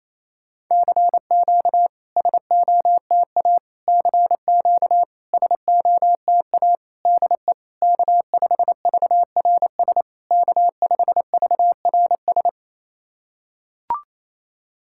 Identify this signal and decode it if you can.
Morse code, heard as a keyed tone, about 28 wpm, 690 Hz. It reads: CQ SOTA CQ SOTA DE K5VRS K5VRS
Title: SOTA QSO Practice 28wpm